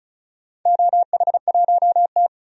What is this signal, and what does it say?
Morse code, heard as a keyed tone, about 34 wpm, 700 Hz. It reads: OH1T
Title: DXCC-100 Call Signs - 4 Characters - 35wpm